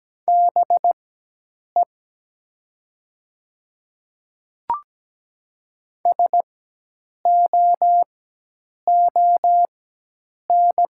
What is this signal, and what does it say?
Morse code, heard as a keyed tone, about 17 wpm, spaced as for 8 wpm, 700 Hz. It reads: BE SOON
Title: 4x Character Spacing - Top 500 Words - 17wpm